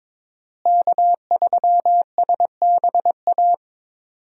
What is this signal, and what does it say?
Morse code, heard as a keyed tone, about 22 wpm, 700 Hz. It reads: K3SBA
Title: USA Call Signs - 5 Characters - Repeated 2x - 22wpm